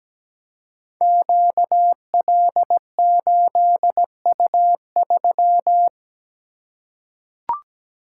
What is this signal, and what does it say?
Morse code, heard as a keyed tone, about 17 wpm, 700 Hz. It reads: QL8U3
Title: Code Groups - 5 Letter:Numbers - 17wpm